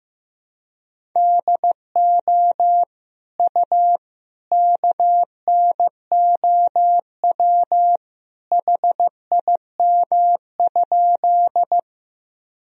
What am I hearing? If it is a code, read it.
Morse code, heard as a keyed tone, about 15 wpm, 700 Hz. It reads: DO U KNOW HIM?
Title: Sentences from Top 100 Words and Abbreviations 15wpm